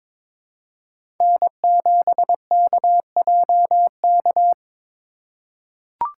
Morse code, heard as a keyed tone, about 22 wpm, 700 Hz. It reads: N7KJK
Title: USA Call Signs - 5 Characters - 2x Character Spacing - 22wpm